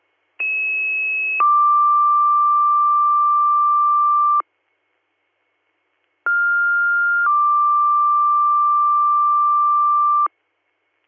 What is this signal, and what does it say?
Two-tone page, heard as a one-second tone, then a three-second tone, about 2600 then 1180 Hz, about 1400 then 1180 Hz. It is heard from 0.4 to 10.3 s.